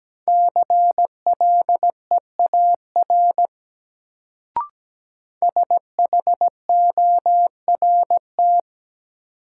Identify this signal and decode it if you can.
Morse code, heard as a keyed tone, about 17 wpm, 700 Hz. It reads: CLEAR SHORT
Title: Sets of 5 Words Encore - New Word Drill - 17wpm